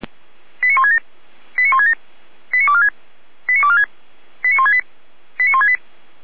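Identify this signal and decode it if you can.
Five-tone selective call (ZVEI), heard as quick runs of five tones, tones of about 70 ms. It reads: 89167 89167 89246 89246 89168 89168
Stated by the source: ZVEI five-tone selective call received via SDR, code 89223, feuerwehr-orlamuende fire brigade.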